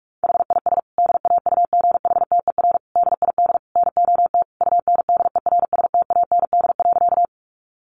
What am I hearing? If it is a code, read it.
Morse code, heard as a keyed tone, about 44 wpm, 700 Hz. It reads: HIS DAUGHTER DID NOT UNDERSTAND.
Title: Sentences from Top 1500 Words 45wpm